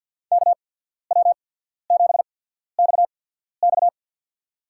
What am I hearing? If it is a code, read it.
Morse code, heard as a keyed tone, about 54 wpm, spaced as for 14 wpm, 700 Hz. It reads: KW7XX
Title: USA Call Signs - 5 Characters - 8x Character Spacing - 50wpm